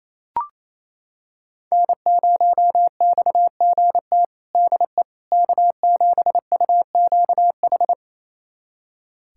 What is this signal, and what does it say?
Morse code, heard as a keyed tone, about 28 wpm, 700 Hz. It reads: N0XGT DE K7UQH
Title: USA Callsign DE Callsign - 5 Characters - 28wpm